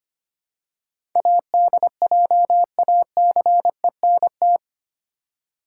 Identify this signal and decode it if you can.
Morse code, heard as a keyed tone, about 25 wpm, 700 Hz. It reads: ADJACENT
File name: Eight Letter Words - 25wpm